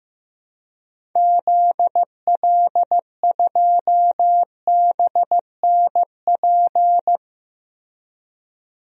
Morse code, heard as a keyed tone, about 15 wpm, 700 Hz. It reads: ZL2BNP
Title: DXCC-100 Call Signs - 6 Characters - 15wpm